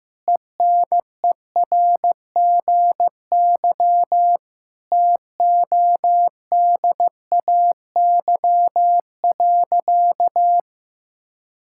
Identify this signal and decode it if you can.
Morse code, heard as a keyed tone, about 15 wpm, 700 Hz. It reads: ENERGY TODAY.
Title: Sentences from Top 1000 Words - Review Part 6 - 15wpm